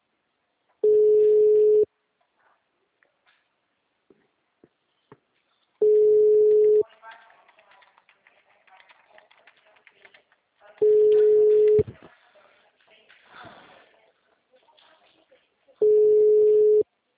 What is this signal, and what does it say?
Ringback tone (Europe, 425 Hz), with speech.
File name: out- -2005- - - .53307